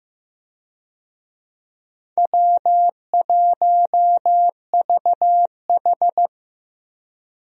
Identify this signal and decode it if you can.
Morse code, heard as a keyed tone, about 15 wpm, 700 Hz. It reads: W1VH
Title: USA Call Signs - 4 Characters - Repeated 3x - 15wpm